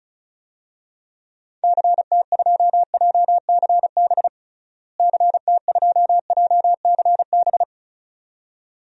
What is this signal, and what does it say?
Morse code, heard as a keyed tone, about 34 wpm, 700 Hz. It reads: CT2JCB CT2JCB
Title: DXCC-100 Call Signs - 6 Characters - Repeated 2x - 35wpm